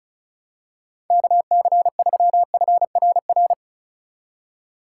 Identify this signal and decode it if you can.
Morse code, heard as a keyed tone, about 36 wpm, 700 Hz. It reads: KC3FRR
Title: USA Call Signs - 6 Characters - 3x Character Spacing - 35wpm